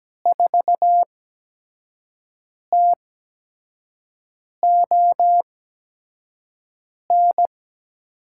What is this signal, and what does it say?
Morse code, heard as a keyed tone, about 17 wpm, spaced as for 5 wpm, 700 Hz. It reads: 4TON
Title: USA Call Signs - 6 Characters - 8x Character Spacing - 17wpm